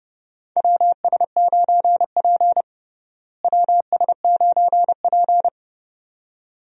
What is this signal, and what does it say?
Morse code, heard as a keyed tone, about 30 wpm, 700 Hz. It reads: WS9P WS9P
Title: USA Call Signs - 4 Characters - Repeated 2x - 30wpm